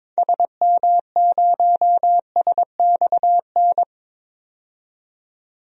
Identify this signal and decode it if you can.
Morse code, heard as a keyed tone, about 22 wpm, 700 Hz. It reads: SM0SXN
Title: DXCC-100 Call Signs - 6 Characters - 22wpm